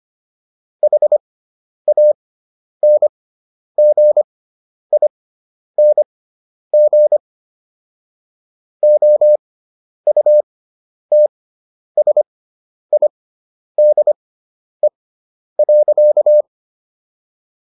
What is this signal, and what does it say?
Morse code, heard as a keyed tone, about 25 wpm, spaced as for 10 wpm, 600 Hz. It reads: HANGING OUTSIDE.